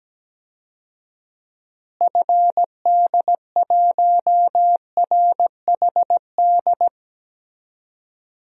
Morse code, heard as a keyed tone, about 17 wpm, 700 Hz. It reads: FD1RHD
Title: DXCC-100 Call Signs - 6 Characters - Repeated 2x - 17wpm